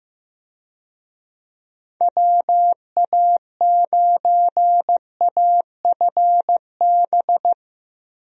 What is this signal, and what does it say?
Morse code, heard as a keyed tone, about 15 wpm, 700 Hz. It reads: WA9AFB
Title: USA Call Signs - 6 Characters - Repeated 3x - 15wpm